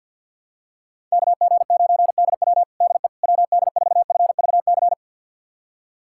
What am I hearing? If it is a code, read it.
Morse code, heard as a keyed tone, about 51 wpm, 700 Hz. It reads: KG9DW DE WD4FVC
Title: USA Callsign DE Callsign 50wpm